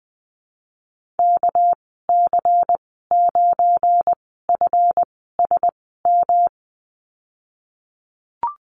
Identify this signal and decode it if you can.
Morse code, heard as a keyed tone, about 20 wpm, spaced as for 15 wpm, 700 Hz. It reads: KC9FSM
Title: 219 Lesson 35g - Number 0 - Call Signs 20wpm